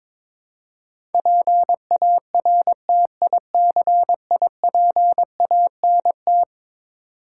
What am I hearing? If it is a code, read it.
Morse code, heard as a keyed tone, about 22 wpm, 700 Hz. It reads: PARTICIPANT
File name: practice / 22wpm / Eleven Letter Words - 22wpm